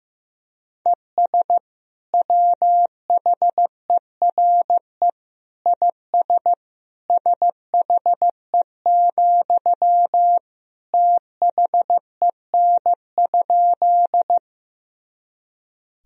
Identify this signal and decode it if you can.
Morse code, heard as a keyed tone, about 15 wpm, 700 Hz. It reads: ES WHERE IS SHE, THEN?
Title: Sentences from Top 200 Words and Abbreviations 15wpm